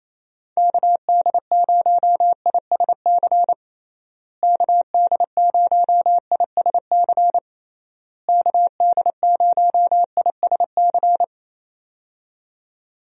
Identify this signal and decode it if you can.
Morse code, heard as a keyed tone, about 27 wpm, 700 Hz. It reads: KD0ISC KD0ISC KD0ISC
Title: USA Call Signs - 6 Characters - Repeated 3x - 28wpm